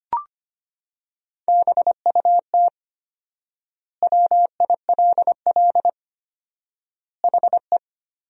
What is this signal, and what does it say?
Morse code, heard as a keyed tone, about 25 wpm, 700 Hz. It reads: BUT WILL HE
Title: 4x Word-Spacing - Sentences from Top 100 Words - 25wpm